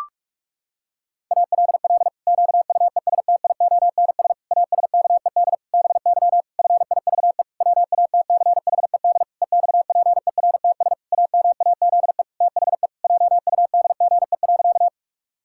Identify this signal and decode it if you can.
Morse code, heard as a keyed tone, about 46 wpm, 700 Hz. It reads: ALL QUESTIONS ASKED BY FIVE WATCHED EXPERTS AMAZE THE JUDGE.
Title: Pangram Challenge v2 - 45wpm